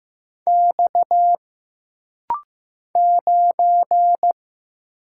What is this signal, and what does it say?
Morse code, heard as a keyed tone, about 15 wpm, 700 Hz. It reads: X 9
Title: Single Letter-Number Alt - Rapid-Fire 15wpm